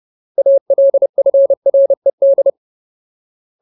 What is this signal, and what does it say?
Morse code, heard as a keyed tone, about 30 wpm, 550 Hz. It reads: ALFRED
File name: Ultimate Word List / 30WPM Ultimate